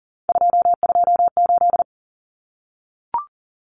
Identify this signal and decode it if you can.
Morse code, heard as a keyed tone, about 40 wpm, 700 Hz. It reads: 228
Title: Common Cut Numbers - 3-Digit 40wpm